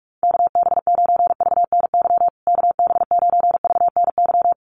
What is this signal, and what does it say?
Morse code, heard as a keyed tone, about 44 wpm, 700 Hz. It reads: KB9VNY KB9VNY
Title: USA Call Signs Repeated 3x 45wpm